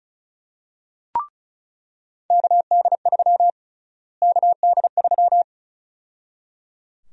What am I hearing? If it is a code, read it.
Morse code, heard as a keyed tone, about 36 wpm, 700 Hz. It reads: KD3 KD3
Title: USA Call Sign Prefixes - Repeated 2x - 35wpm